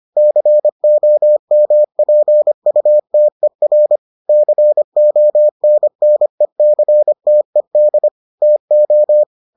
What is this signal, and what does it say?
Morse code, heard as a keyed tone, about 25 wpm, 600 Hz. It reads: COMPUTER CONNECTED TO